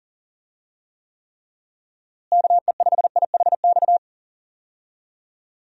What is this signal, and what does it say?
Morse code, heard as a keyed tone, about 39 wpm, 700 Hz. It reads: KE5IHX
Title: USA Call Signs - 6 Characters - Repeated 3x - 40wpm